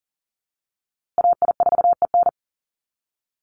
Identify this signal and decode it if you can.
Morse code, heard as a keyed tone, about 40 wpm, 700 Hz. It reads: AI4EN
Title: USA Call Signs Repeated 2x 40wpm